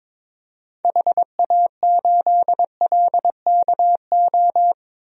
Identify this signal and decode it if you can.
Morse code, heard as a keyed tone, about 22 wpm, 700 Hz. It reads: HA8LKO
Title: DXCC-100 Call Signs - 6 Characters - 22wpm